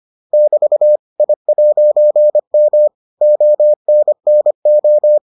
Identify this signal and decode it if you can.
Morse code, heard as a keyed tone, about 25 wpm, 600 Hz. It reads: <BT> I'M ONNO